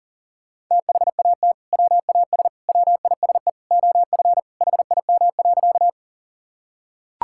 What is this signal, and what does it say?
Morse code, heard as a keyed tone, about 38 wpm, 700 Hz. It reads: THAT WAS WISE OF HIM.
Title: Sentences from Top 1600 Words 40wpm